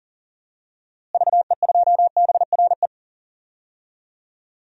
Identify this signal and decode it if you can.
Morse code, heard as a keyed tone, about 39 wpm, 700 Hz. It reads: VE2BRE
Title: DXCC-100 Call Signs - 6 Characters - Repeated 3x - 40wpm